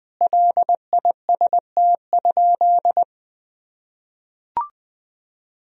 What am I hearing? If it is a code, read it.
Morse code, heard as a keyed tone, about 20 wpm, 700 Hz. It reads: LIST?
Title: Sentences from Top 1000 Words - Review Part 2 - 20wpm